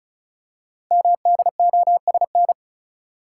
Morse code, heard as a keyed tone, about 36 wpm, 690 Hz. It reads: MDOSN
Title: Code Groups - 5 Letter:Numbers - 35wpm